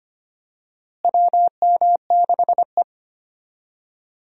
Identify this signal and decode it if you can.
Morse code, heard as a keyed tone, about 25 wpm, 700 Hz. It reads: WM6E
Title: USA Call Signs - 4 Characters - Repeated 2x - 25wpm